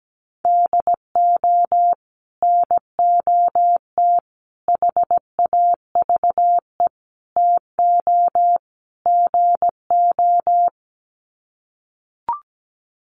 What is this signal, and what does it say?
Morse code, heard as a keyed tone, about 17 wpm, 700 Hz. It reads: DO NOT HAVE TO GO
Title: Sets of 5 Words 17wpm